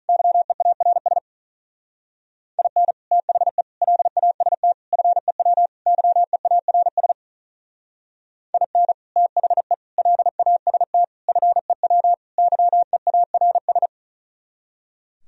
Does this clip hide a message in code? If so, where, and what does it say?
Yes: Morse code from 0.1 to 13.9 s, heard as a keyed tone, about 45 wpm, 700 Hz. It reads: YEARS IN THE LAST FEW YEARS IN THE LAST FEW YEARS